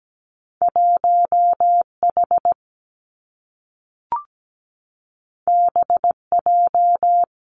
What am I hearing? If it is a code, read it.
Morse code, heard as a keyed tone, about 17 wpm, 700 Hz. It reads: 1H BJ
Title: Two Challenging Letter-Number Combinations 17wpm